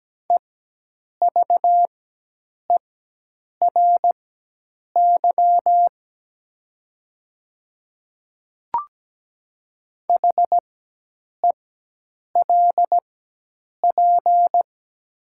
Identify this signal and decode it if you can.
Morse code, heard as a keyed tone, about 17 wpm, spaced as for 8 wpm, 700 Hz. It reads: EVERY HELP
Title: 4x Character Spacing - Top 200 Words - 17wpm